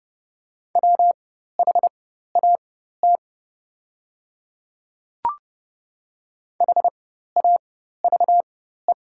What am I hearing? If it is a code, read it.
Morse code, heard as a keyed tone, about 30 wpm, spaced as for 14 wpm, 690 Hz. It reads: WHAT HAVE
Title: 4x Character Spacing - Top 100 Words - 30wpm